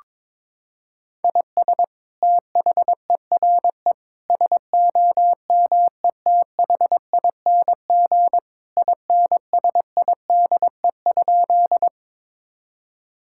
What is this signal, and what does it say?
Morse code, heard as a keyed tone, about 22 wpm, 700 Hz. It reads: IS THERE SOMETHING INSIDE?
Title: Sentences from Top 1000 Words - Review Part 6 - 22wpm